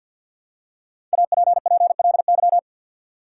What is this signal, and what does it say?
Morse code, heard as a keyed tone, about 52 wpm, 700 Hz. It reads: APPLY
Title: Five Letter Words - 50wpm